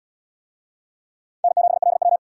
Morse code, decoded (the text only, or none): N7LW